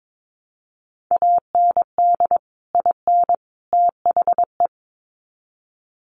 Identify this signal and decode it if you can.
Morse code, heard as a keyed tone, about 22 wpm, 700 Hz. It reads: AND IN THE
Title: Sets of 3 Words 22wpm